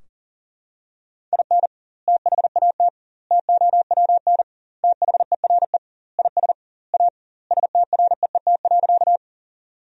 Morse code, heard as a keyed tone, about 40 wpm, 700 Hz. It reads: IN THAT TOWN THERE IS A STREET.
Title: 2x Word-Spacing - Sentences from Top 500 Words - 40wpm